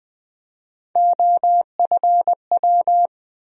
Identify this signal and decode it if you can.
Morse code, heard as a keyed tone, about 20 wpm, 700 Hz. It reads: OFW
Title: USA Call Sign Postfixes - 3 Letters 20wpm